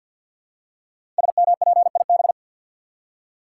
Morse code, heard as a keyed tone, about 50 wpm, 700 Hz. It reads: SMPIB
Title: Code Groups - 5 Letters - 50wpm